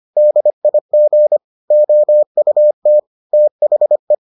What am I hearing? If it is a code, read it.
Morse code, heard as a keyed tone, about 25 wpm, 600 Hz. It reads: DIG OUT THE